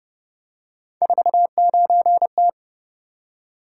Morse code, heard as a keyed tone, about 29 wpm, 700 Hz. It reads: 49T